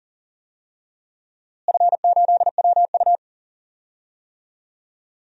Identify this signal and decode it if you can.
Morse code, heard as a keyed tone, about 37 wpm, 700 Hz. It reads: F8WU
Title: DXCC-100 Call Signs - 4 Characters - Repeated 3x - 40wpm